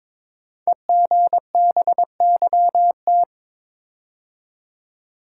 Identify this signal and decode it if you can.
Morse code, heard as a keyed tone, about 22 wpm, 700 Hz. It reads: EGBYT
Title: Code Groups - 5 Letter:Numbers - 22wpm